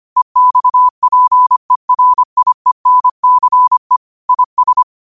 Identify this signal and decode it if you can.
Morse code, heard as a keyed tone, about 25 wpm, 1000 Hz. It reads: EXPERIENCE IS
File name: qotdfiles / 25wpm